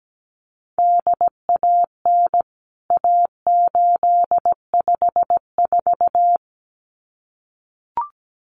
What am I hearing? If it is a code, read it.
Morse code, heard as a keyed tone, about 17 wpm, 700 Hz. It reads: DAN A854
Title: CWT Exchange 17wpm